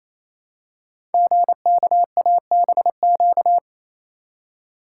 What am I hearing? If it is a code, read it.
Morse code, heard as a keyed tone, about 29 wpm, 700 Hz. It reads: GKABQ
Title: Code Groups - 5 Letters - 28wpm